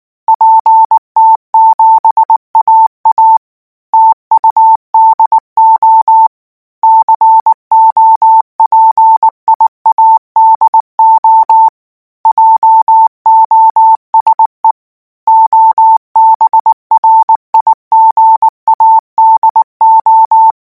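Morse code, heard as a keyed tone, about 19 wpm, 890 Hz. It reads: PT7AA TUDO COPIADO JOSE OBRIGADO
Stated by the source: Morse code practice text at 18 wpm, from Portal do CW.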